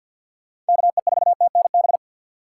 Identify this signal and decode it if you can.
Morse code, heard as a keyed tone, about 52 wpm, 700 Hz. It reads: KE4TNB
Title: USA Call Signs - 2x Character Spacing - 50wpm